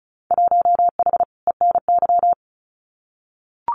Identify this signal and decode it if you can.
Morse code, heard as a keyed tone, about 34 wpm, 700 Hz. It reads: 1H ENY
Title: Winter Field Day Exchanges 35wpm